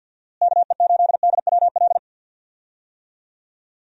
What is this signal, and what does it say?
Morse code, heard as a keyed tone, about 50 wpm, 700 Hz. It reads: KE8DWL
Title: USA Call Signs - 6 Characters - Repeated 3x - 50wpm